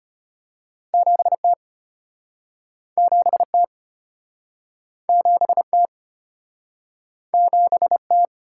Morse code, heard as a keyed tone, about 34 wpm, 700 Hz. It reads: 7T 7T 7T 7T